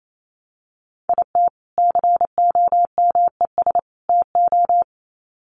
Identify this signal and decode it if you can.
Morse code, heard as a keyed tone, about 28 wpm, 700 Hz. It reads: IT COMES TO
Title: Sets of 3 Words 28wpm